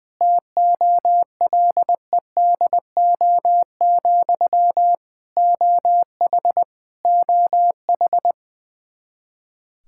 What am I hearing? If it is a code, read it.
Morse code, heard as a keyed tone, about 20 wpm, 700 Hz. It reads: TOLEDO, OH OH
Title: Practice QSO - Exchange 1 - Beginning - 20wpm